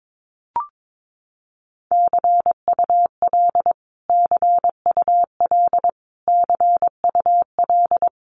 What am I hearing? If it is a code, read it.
Morse code, heard as a keyed tone, about 22 wpm, 700 Hz. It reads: CUL CUL CUL
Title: Abbreviations Repeated 3x - 22wpm